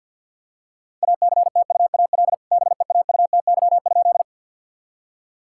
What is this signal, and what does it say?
Morse code, heard as a keyed tone, about 50 wpm, 700 Hz. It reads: ACTUAL BEAUTY?
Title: Sentences from Top 1000 Words - Review Part 5 - 50wpm